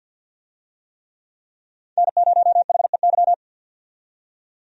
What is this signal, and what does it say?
Morse code, heard as a keyed tone, about 53 wpm, 700 Hz. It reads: N0HEY
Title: USA Call Signs - 5 Characters - 50wpm